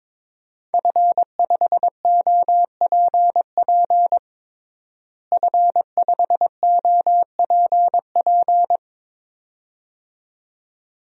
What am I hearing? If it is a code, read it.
Morse code, heard as a keyed tone, about 22 wpm, 700 Hz. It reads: F5OPP F5OPP
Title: DXCC-100 Call Signs - 5 Characters - Repeated 2x - 22wpm